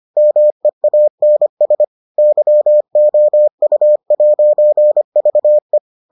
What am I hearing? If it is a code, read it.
Morse code, heard as a keyed tone, about 25 wpm, 600 Hz. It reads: MEANS YOU'VE